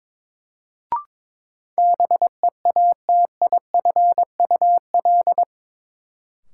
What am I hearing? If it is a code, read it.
Morse code, heard as a keyed tone, about 22 wpm, 700 Hz. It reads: BEATIFUL
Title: Sentences from Top 1300 Words - New Word Drill - 22wpm